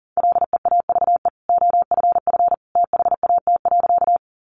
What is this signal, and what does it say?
Morse code, heard as a keyed tone, about 40 wpm, 700 Hz. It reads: LEAVE OFF THAT.
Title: Sentences from Top 400 Words 40wpm